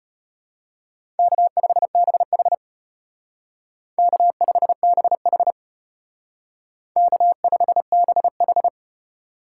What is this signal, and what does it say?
Morse code, heard as a keyed tone, about 33 wpm, 700 Hz. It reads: K5BH K5BH K5BH